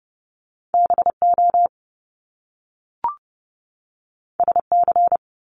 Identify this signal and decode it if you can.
Morse code, heard as a keyed tone, about 30 wpm, 700 Hz. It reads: BO SC
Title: Top Two Letter Combinations in Words 30wpm